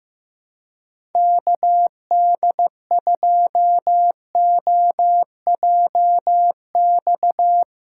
Morse code, heard as a keyed tone, about 15 wpm, 700 Hz. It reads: KD2OJX
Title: USA Call Signs - 2x Character Spacing - 15wpm